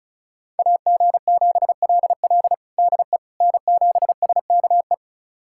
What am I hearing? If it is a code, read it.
Morse code, heard as a keyed tone, about 34 wpm, 700 Hz. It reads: AG7LL DE N7SKE
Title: USA Callsign DE Callsign - 5 Characters - 35wpm